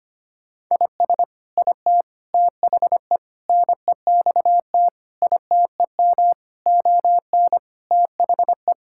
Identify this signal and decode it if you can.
Morse code, heard as a keyed tone, about 25 wpm, 700 Hz. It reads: IS IT THE NEXT ITEM ON THE
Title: Sentences from Top 1000 Words - Review Part 6 - 25wpm